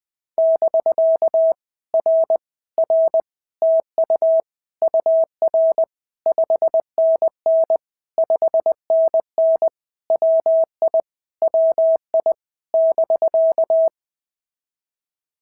Morse code, heard as a keyed tone, about 22 wpm, 650 Hz. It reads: <BK> R R TU UR 5NN 5NN WI WI <BK>
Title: POTA QSO Practice - 20wpm